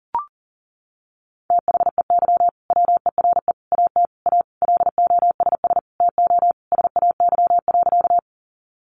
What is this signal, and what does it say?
Morse code, heard as a keyed tone, about 40 wpm, 700 Hz. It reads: THEY WERE AT A LOSS TO SAY.
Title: Sentences from Top 1400 Words 40wpm